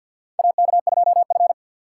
Morse code, heard as a keyed tone, about 46 wpm, 700 Hz. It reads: AK2F
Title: USA Call Signs - 4 Characters - 2x Character Spacing - 50wpm